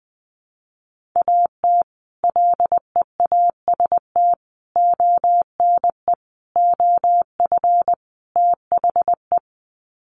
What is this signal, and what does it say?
Morse code, heard as a keyed tone, about 20 wpm, 700 Hz. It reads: AT LEAST ONE OF THE
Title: Sets of 5 Words Encore 20wpm